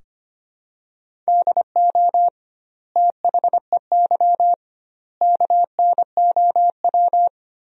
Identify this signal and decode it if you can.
Morse code, heard as a keyed tone, about 25 wpm, 700 Hz. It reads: DO THEY KNOW
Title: 2x Word-Spacing - Sentences from Top 100 Words - 25wpm